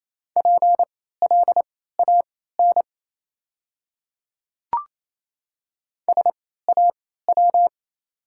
Morse code, heard as a keyed tone, about 28 wpm, spaced as for 16 wpm, 700 Hz. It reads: PLAN SAW
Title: 3x Character Spacing - Top 500 Words - 28wpm